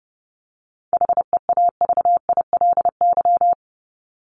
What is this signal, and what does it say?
Morse code, heard as a keyed tone, about 30 wpm, 700 Hz. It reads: HEAVILY
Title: LY Suffix - Top 100 Words with Base Words 30wpm